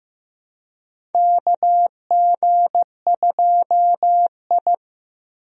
Morse code, heard as a keyed tone, about 15 wpm, 700 Hz. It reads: KG2I
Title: USA Call Signs - 4 Characters - 15wpm